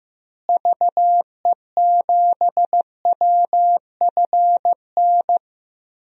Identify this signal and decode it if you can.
Morse code, heard as a keyed tone, about 16 wpm, 700 Hz. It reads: VE7WFN
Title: DXCC-100 Call Signs - 6 Characters - Repeated 2x - 15wpm